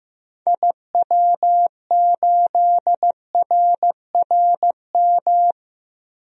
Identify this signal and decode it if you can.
Morse code, heard as a keyed tone, about 15 wpm, 700 Hz. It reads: IW8RRM
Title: DXCC-100 Call Signs - 6 Characters - 15wpm